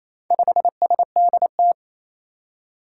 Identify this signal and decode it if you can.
Morse code, heard as a keyed tone, about 28 wpm, 700 Hz. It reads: 5SDT
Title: Four Challenging Letter-Number Combinations 28wpm